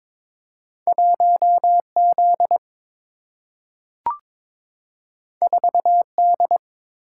Morse code, heard as a keyed tone, about 22 wpm, 700 Hz. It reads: 1 Z 4 D
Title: Single-Letter Single-Number 22wpm